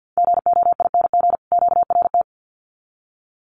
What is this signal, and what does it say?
Morse code, heard as a keyed tone, about 50 wpm, 700 Hz. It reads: GOING QRT
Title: Common QSO Phrases 50wpm